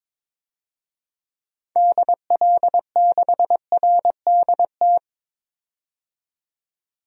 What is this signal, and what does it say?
Morse code, heard as a keyed tone, about 22 wpm, 700 Hz. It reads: DL6RDT
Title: DXCC-100 Call Signs - 6 Characters - 22wpm